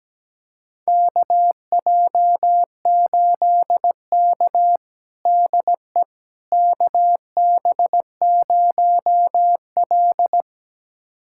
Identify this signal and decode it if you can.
Morse code, heard as a keyed tone, about 17 wpm, 700 Hz. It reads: KJ8K DE KB0L
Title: USA Callsign DE Callsign - 4 Characters - 17wpm